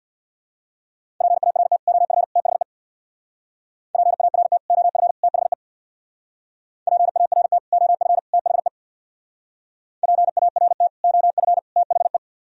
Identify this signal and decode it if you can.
Morse code, heard as a keyed tone, about 62 wpm, 700 Hz. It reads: PART OF THE PART OF THE PART OF THE PART OF THE
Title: Speed-Racing - Sets of 3 Words 50wpm